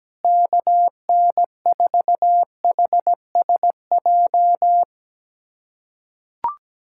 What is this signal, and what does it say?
Morse code, heard as a keyed tone, about 17 wpm, 700 Hz. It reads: KN4HSJ
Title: USA Callsign DE Callsign 17wpm